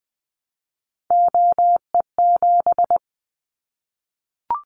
Morse code, heard as a keyed tone, about 20 wpm, 700 Hz. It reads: OE7
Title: Callsign Prefixes for Top 100 DXCC - 20wpm